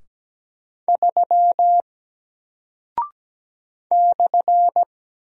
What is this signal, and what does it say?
Morse code, heard as a keyed tone, about 17 wpm, 700 Hz. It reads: 3 /
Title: Single Character (Letter:Number:Punctuation:Prosign) 17wpm